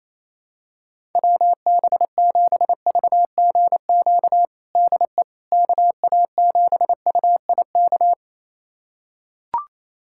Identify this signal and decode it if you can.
Morse code, heard as a keyed tone, about 28 wpm, 700 Hz. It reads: WB7VGQ DE KA7UIK
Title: USA Callsign DE Callsign - 6 Characters - 28wpm